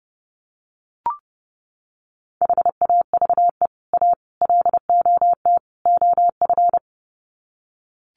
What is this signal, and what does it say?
Morse code, heard as a keyed tone, about 30 wpm, 700 Hz. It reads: HAVE A LOT OF
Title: Sets of 4 Words 30wpm